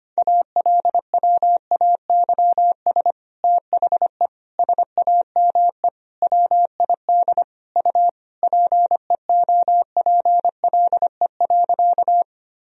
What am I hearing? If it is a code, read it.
Morse code, heard as a keyed tone, about 25 wpm, 700 Hz. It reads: ALWAYS THE SAME WID U PEOPLE.
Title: Sentences from Top 300 Words and Abbreviations 25wpm